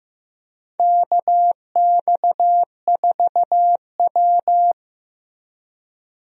Morse code, heard as a keyed tone, about 16 wpm, 700 Hz. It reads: KX4W
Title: USA Call Signs - 4 Characters - 15wpm